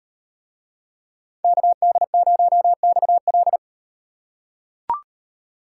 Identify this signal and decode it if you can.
Morse code, heard as a keyed tone, about 38 wpm, 700 Hz. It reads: KD0XL